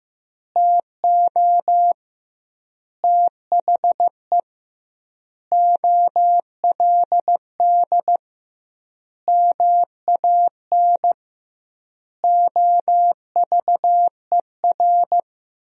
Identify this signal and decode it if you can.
Morse code, heard as a keyed tone, about 15 wpm, 700 Hz. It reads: TO THE OLD MAN OVER
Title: 2x Word-Spacing - Sentences from Top 200 Words - 15wpm